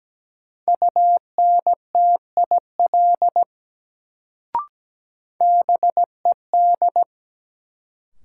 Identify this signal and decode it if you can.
Morse code, heard as a keyed tone, about 17 wpm, 700 Hz. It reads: UNTIL BED
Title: Sentences from Top 500 Words - New Word Drill - 17wpm